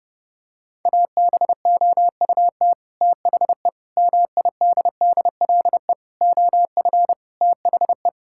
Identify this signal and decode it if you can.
Morse code, heard as a keyed tone, about 32 wpm, 700 Hz. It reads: ABOUT THE MIDDLE OF THE
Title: Sentences from Top 1000 Words - Review Part 6 - 30wpm